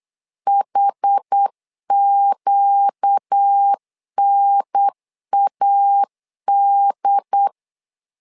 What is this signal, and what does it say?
Morse code, heard as a keyed tone, about 8 wpm, 800 Hz. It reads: HQNAD
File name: LEZIONI / 5 LBVC / 5 L 8wpm 2831